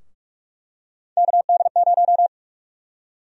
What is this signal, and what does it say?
Morse code, heard as a keyed tone, about 46 wpm, 700 Hz. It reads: KD0
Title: USA Call Sign Prefixes - 45wpm